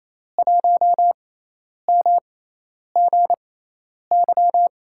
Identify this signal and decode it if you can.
Morse code, heard as a keyed tone, about 28 wpm, 700 Hz. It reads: 1 M G Y
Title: USA Call Signs - 6x Character Spacing - 28wpm